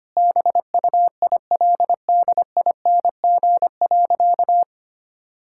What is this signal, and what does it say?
Morse code, heard as a keyed tone, about 25 wpm, 700 Hz. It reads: BUILDING.
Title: Sentences from Top 1000 Words - Review Part 5 - 25wpm